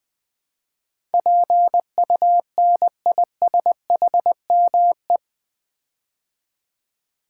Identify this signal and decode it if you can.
Morse code, heard as a keyed tone, about 20 wpm, 700 Hz. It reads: PUNISHME
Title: Build A Word - 4 Letters At A Time - 10-Letter Words - 20wpm